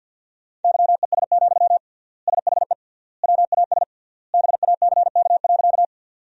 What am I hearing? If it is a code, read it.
Morse code, heard as a keyed tone, about 53 wpm, 700 Hz. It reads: YES, SHE WAS BACK.
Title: 3x Word-Spacing - Sentences from Top 500 Words - 50wpm